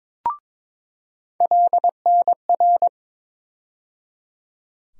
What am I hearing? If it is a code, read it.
Morse code, heard as a keyed tone, about 22 wpm, 700 Hz. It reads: LNR
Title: USA Call Sign Postfixes - 3 Letters 22wpm